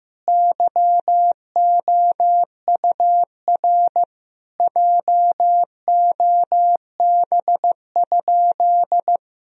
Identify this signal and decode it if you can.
Morse code, heard as a keyed tone, about 15 wpm, 700 Hz. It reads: YOUR JOB?
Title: Sentences from Top 1000 Words - Review Part 3 - 15wpm